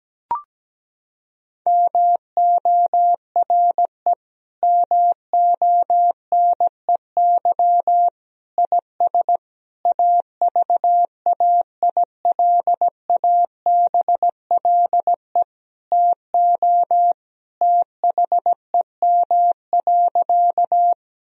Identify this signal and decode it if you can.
Morse code, heard as a keyed tone, about 17 wpm, 700 Hz. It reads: MORE MONEY IS AVAILABLE TO THEM.
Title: Sentences from Top 1000 Words - Review Part 3 - 17wpm